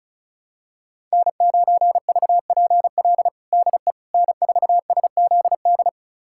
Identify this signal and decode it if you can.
Morse code, heard as a keyed tone, about 34 wpm, 700 Hz. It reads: N9VPL DE N4SZD
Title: USA Callsign DE Callsign - 5 Characters - 35wpm